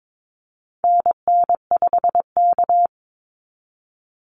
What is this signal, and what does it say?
Morse code, heard as a keyed tone, about 22 wpm, 700 Hz. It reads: NN5K
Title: NA SOTA Activator Callsigns 22wpm